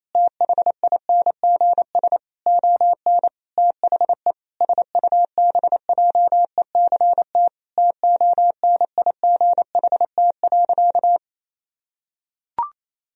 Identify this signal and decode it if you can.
Morse code, heard as a keyed tone, about 28 wpm, 700 Hz. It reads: THINGS ON THE SUBJECT TONIGHT.
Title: Sentences from Top 1000 Words - Review Part 4 - 28wpm